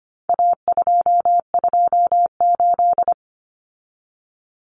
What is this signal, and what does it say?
Morse code, heard as a keyed tone, about 25 wpm, 700 Hz. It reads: A228
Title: Common Cut Numbers - 4-Digit 25wpm